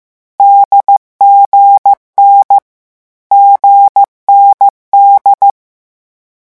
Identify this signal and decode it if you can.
Morse code, heard as a keyed tone, about 15 wpm, 800 Hz. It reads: DGN GND
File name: morse mp3 / lecon4-vitesse 14